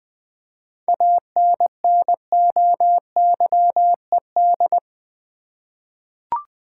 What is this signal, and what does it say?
Morse code, heard as a keyed tone, about 20 wpm, 700 Hz. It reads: ANNOYED
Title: 7 Letter Words - First 3 Letters Given - 20wpm